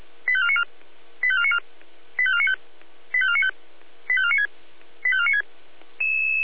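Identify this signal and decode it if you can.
Five-tone selective call (ZVEI), heard as quick runs of five tones, tones of about 80 ms. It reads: 85494 85494 85495 85495 85496 85496
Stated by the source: ZVEI five-tone selective call 85492, received on SDR.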